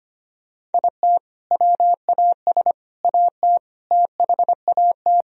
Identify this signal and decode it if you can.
Morse code, heard as a keyed tone, about 25 wpm, 700 Hz. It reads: IT WAS AT THAT
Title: Sentences from Top 1000 Words - Review Part 3 - 25wpm